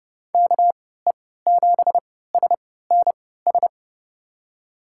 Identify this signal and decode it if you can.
Morse code, heard as a keyed tone, about 30 wpm, spaced as for 17 wpm, 690 Hz. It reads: KE7SNS
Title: USA Call Signs - 3x Character Spacing - 30wpm